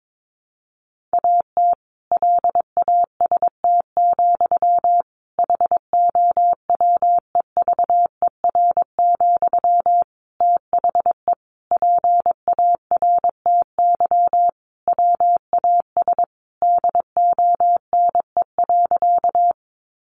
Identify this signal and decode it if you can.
Morse code, heard as a keyed tone, about 22 wpm, 700 Hz. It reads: AT LAST, HOWEVER, THE PARTY WAS DONE.
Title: Sentences from Top 1400 Words 22wpm